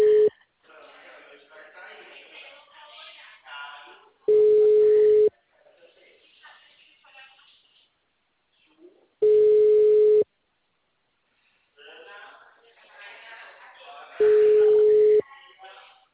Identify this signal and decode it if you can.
Ringback tone (Europe, 425 Hz), with speech.